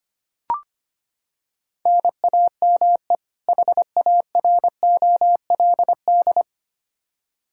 Morse code, heard as a keyed tone, about 25 wpm, 700 Hz. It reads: NAME HAROLD
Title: Practice QSO - Exchange 1 - Beginning - 25wpm